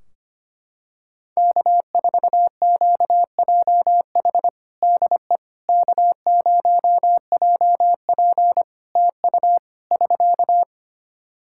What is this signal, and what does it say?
Morse code, heard as a keyed tone, about 25 wpm, 700 Hz. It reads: K4QJH DE K0JP TU <SK>
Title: Practice QSO - Exchange 4 - Ending - 25wpm